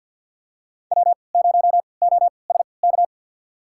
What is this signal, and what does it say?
Morse code, heard as a keyed tone, about 52 wpm, spaced as for 29 wpm, 700 Hz. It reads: W0OSK
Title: USA Call Signs - 3x Character Spacing - 50wpm